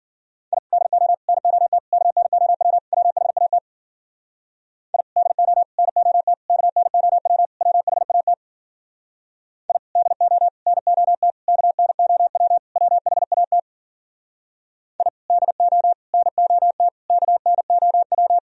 Morse code, heard as a keyed tone, about 52 wpm, 700 Hz. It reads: I DO NOT KNOW WHAT I DO NOT KNOW WHAT I DO NOT KNOW WHAT I DO NOT KNOW